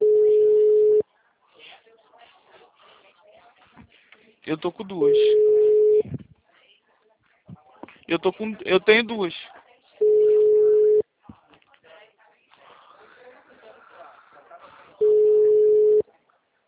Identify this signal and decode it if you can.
Ringback tone (Europe, 425 Hz), with speech.